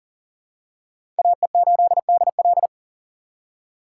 Morse code, heard as a keyed tone, about 40 wpm, 700 Hz. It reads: AE8DL
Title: USA Call Signs - 5 Characters - 2x Character Spacing - 40wpm